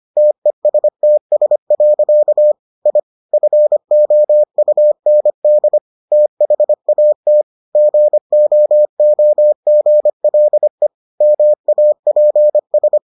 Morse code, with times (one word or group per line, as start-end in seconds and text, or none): 0.2-2.5 TESTS.
2.9-3.0 I
3.3-5.8 FOUND
6.1-7.4 THAT
7.8-10.9 GOOGLE
11.2-13.0 MAPS